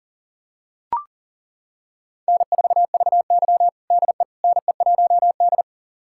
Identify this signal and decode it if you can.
Morse code, heard as a keyed tone, about 40 wpm, 690 Hz. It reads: N4VY DE NE1D
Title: USA Callsign DE Callsign - 4 Characters - 40wpm